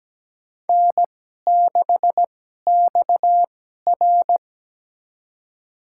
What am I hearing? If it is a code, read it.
Morse code, heard as a keyed tone, about 17 wpm, spaced as for 12 wpm, 700 Hz. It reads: N6XR
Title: USA Call Signs - 4 Characters - 2x Character Spacing - 17wpm